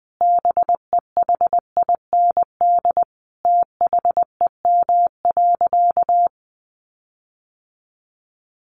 Morse code, heard as a keyed tone, about 20 wpm, 700 Hz. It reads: BEHIND THEM.
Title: Sentences from Top 1400 Words 20wpm